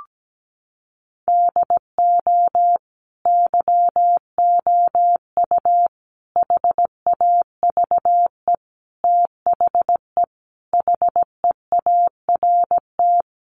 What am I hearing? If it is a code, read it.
Morse code, heard as a keyed tone, about 17 wpm, 700 Hz. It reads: DO YOU HAVE THE HEART
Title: Sentences from Top 1200 Words 17wpm